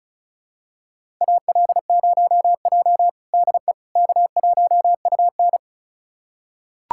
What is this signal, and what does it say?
Morse code, heard as a keyed tone, about 36 wpm, 700 Hz. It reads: AL0J DE K1UN
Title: USA Callsign DE Callsign - 4 Characters - 35wpm